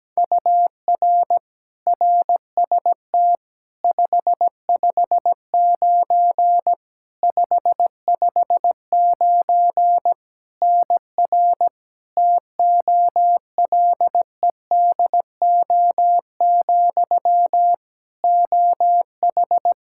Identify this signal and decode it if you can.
Morse code, heard as a keyed tone, about 17 wpm, 700 Hz. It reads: UR RST 559 559 NR TOLEDO, OH
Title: Practice QSO - Exchange 1 - Beginning - 17wpm